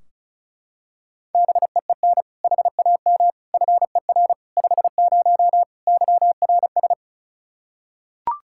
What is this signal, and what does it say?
Morse code, heard as a keyed tone, about 34 wpm, 700 Hz. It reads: BEEN HAM FER 50 YRS
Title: K1USN SST QSO Practice 35wpm